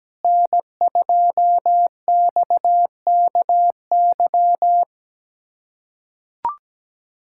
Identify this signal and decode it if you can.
Morse code, heard as a keyed tone, about 17 wpm, 700 Hz. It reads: N2XKY
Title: USA Callsign DE Callsign - 5 Characters - 17wpm